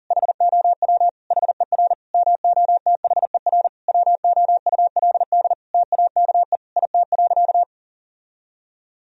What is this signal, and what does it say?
Morse code, heard as a keyed tone, about 40 wpm, 700 Hz. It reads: HOW HER MOTHER WOULD TAKE IT.
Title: Sentences from Top 1000 Words - Review Part 4 - 40wpm